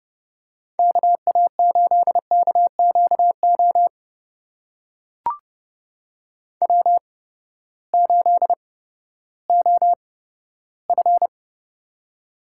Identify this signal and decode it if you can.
Morse code, heard as a keyed tone, about 30 wpm, spaced as for 15 wpm, 700 Hz. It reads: KA8KQO W8OF
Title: USA Call Signs - 8x Character Spacing - 30wpm